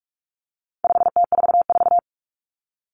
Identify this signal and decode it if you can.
Morse code, heard as a keyed tone, about 44 wpm, 700 Hz. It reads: 5T44
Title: Common Cut Numbers - 4-Digit 45wpm